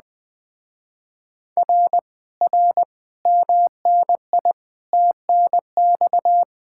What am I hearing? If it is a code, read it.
Morse code, heard as a keyed tone, about 20 wpm, 700 Hz. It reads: R R MNI TNX
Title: Speed-Racing Common QSO Phrases - 20wpm